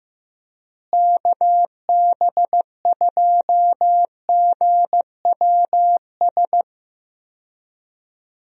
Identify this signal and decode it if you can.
Morse code, heard as a keyed tone, about 15 wpm, 700 Hz. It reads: KB2GWS
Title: USA Call Signs - 6 Characters - Repeated 2x - 15wpm